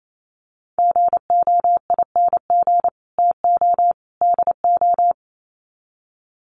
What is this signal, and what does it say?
Morse code, heard as a keyed tone, about 28 wpm, 700 Hz. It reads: GOING TO DO
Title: Sets of 3 Words 28wpm